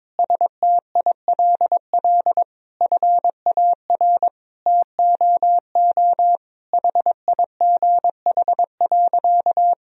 Morse code, heard as a keyed tone, about 22 wpm, 700 Hz. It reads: STILL FAR TOO HIGH.
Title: Sentences from Top 1000 Words - Review Part 1 - 22wpm